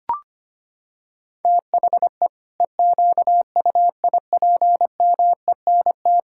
Morse code, heard as a keyed tone, about 25 wpm, 700 Hz. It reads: THE EQUIPMENT
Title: Sentences from Top 1000 Words - Review Part 5 - 25wpm